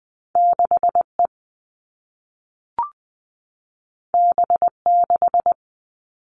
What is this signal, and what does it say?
Morse code, heard as a keyed tone, about 20 wpm, 700 Hz. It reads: 6 E B 6
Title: All Cut Numbers - 2-Digit 20wpm